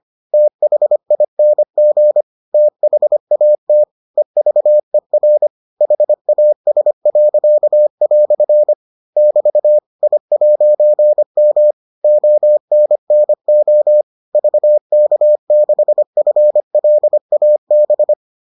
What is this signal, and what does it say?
Morse code, heard as a keyed tone, about 25 wpm, 600 Hz. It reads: THING THAT EVER HAS." <BT> I'M ONNO VK6FLAB